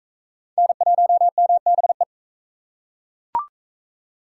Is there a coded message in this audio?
Yes: Morse code from 0.6 to 2.0 s, heard as a keyed tone, about 37 wpm, 700 Hz. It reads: N1MBE